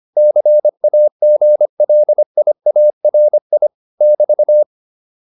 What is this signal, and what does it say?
Morse code, heard as a keyed tone, about 25 wpm, 600 Hz. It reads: CAGLIARI <BT>
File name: QSO409 25